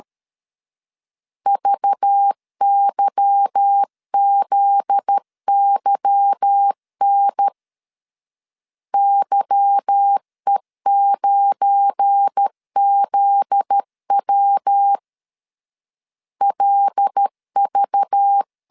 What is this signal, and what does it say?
Morse code, heard as a keyed tone, about 13 wpm, 800 Hz. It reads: VYZYN YE9ZW LV
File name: LEZIONI / 8 YJ45 / 8 Y 12wpm 2228